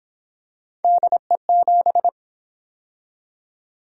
Morse code, heard as a keyed tone, about 29 wpm, 700 Hz. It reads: DE7